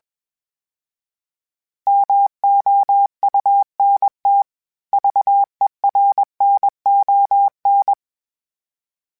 Morse code, heard as a keyed tone, about 21 wpm, 800 Hz. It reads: MOUNT VERNON
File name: fromSMs / FromSMs mp3s / Cities@20